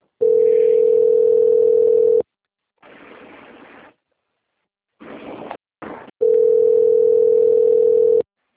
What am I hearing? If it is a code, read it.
Ringback tone (North America), 440 + 480 Hz.
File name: out- - - - - .7131